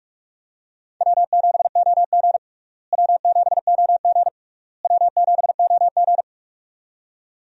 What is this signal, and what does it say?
Morse code, heard as a keyed tone, about 42 wpm, 700 Hz. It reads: W7OG W7OG W7OG
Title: USA Call Signs - 4 Characters - Repeated 3x - 45wpm